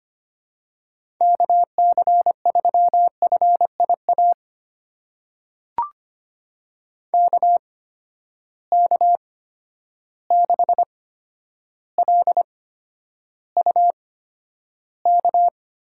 Morse code, heard as a keyed tone, about 25 wpm, spaced as for 11 wpm, 700 Hz. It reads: KC3FIA KK6LUK
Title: USA Call Signs - 6 Characters - 8x Character Spacing - 25wpm